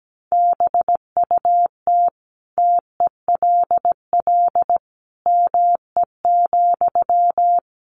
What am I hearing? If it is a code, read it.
Morse code, heard as a keyed tone, about 17 wpm, 700 Hz. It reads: BUT TELL ME,
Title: Sentences from Top 1500 Words 17wpm